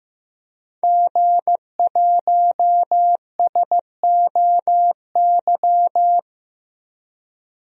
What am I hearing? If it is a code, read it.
Morse code, heard as a keyed tone, about 15 wpm, 700 Hz. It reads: G1SOY
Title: DXCC-100 Call Signs - 5 Characters - Repeated 2x - 15wpm